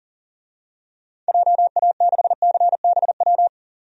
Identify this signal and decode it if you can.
Morse code, heard as a keyed tone, about 40 wpm, 700 Hz. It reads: JA6CBW
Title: DXCC-100 Call Signs - 6 Characters - 40wpm